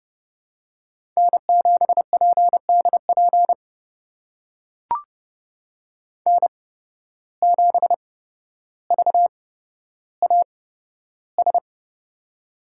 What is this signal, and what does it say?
Morse code, heard as a keyed tone, about 32 wpm, spaced as for 13 wpm, 700 Hz. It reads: N7PDP N7VAS
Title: USA Call Signs - 5 Characters - 8x Character Spacing - 30wpm